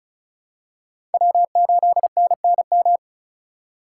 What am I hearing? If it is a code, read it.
Morse code, heard as a keyed tone, about 34 wpm, 700 Hz. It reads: W8NNM
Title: USA Call Signs - 5 Characters - 4x Character Spacing - 35wpm